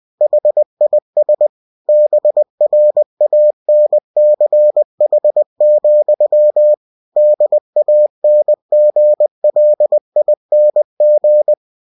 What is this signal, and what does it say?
Morse code, heard as a keyed tone, about 20 wpm, 600 Hz. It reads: HIS BRANCH, DANGLING
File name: cw-books / winnie-the-pooh / winnie-20wpm-page-0094